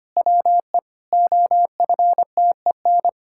Morse code, heard as a keyed tone, about 25 wpm, 700 Hz. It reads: WE OFTEN
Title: Pangram Challenge v2 - 25wpm